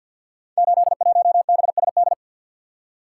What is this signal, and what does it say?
Morse code, heard as a keyed tone, about 46 wpm, 700 Hz. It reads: 81BSD
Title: Code Groups - 5 Letter:Numbers - 50wpm